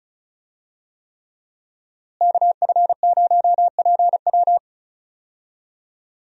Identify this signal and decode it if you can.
Morse code, heard as a keyed tone, about 36 wpm, 700 Hz. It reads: KF0PW
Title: USA Call Signs - 5 Characters - Repeated 3x - 35wpm